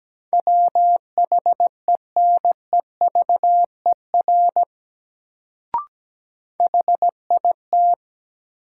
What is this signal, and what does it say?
Morse code, heard as a keyed tone, about 17 wpm, 700 Hz. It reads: WHENEVER HIT
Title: Sentences from Top 1200 Words - New Word Drill - 17wpm